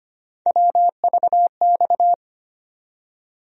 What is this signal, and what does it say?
Morse code, heard as a keyed tone, about 25 wpm, 700 Hz. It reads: WVX
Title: USA Call Sign Postfixes - 3 Letters 25wpm